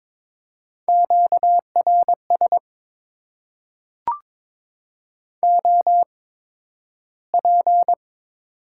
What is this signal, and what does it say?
Morse code, heard as a keyed tone, about 22 wpm, spaced as for 12 wpm, 700 Hz. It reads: QRS OP
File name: Top 90 QSO Elements - 8x Character Spacing - 22wpm